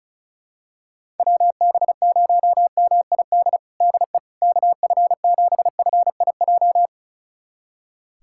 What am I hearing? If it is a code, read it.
Morse code, heard as a keyed tone, about 34 wpm, 700 Hz. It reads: WB0MID DE KF7FIJ
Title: USA Callsign DE Callsign - 6 Characters - 35wpm